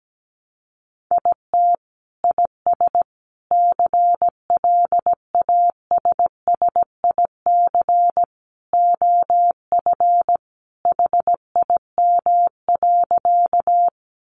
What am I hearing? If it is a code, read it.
Morse code, heard as a keyed tone, about 17 wpm, 700 Hz. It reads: IT IS CLASSIC OF HIM.
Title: Sentences from Top 1000 Words 17wpm